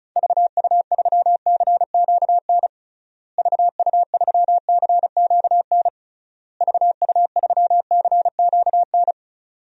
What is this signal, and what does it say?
Morse code, heard as a keyed tone, about 36 wpm, 700 Hz. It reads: VU3CQN VU3CQN VU3CQN
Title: DXCC-100 Call Signs - 6 Characters - Repeated 3x - 35wpm